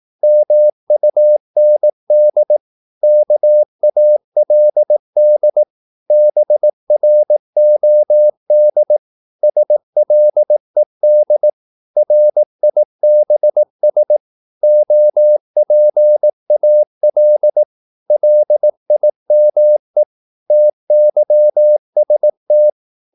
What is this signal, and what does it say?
Morse code, heard as a keyed tone, about 18 wpm, 600 Hz. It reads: MUND KALD BROD SLED RIBS OPAL LIME TYST
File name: Korte ord DK 18wpm